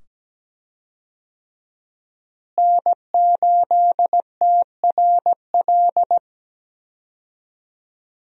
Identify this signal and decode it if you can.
Morse code, heard as a keyed tone, about 17 wpm, 700 Hz. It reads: N8TRL
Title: USA Call Signs - 5 Characters - Repeated 2x - 17wpm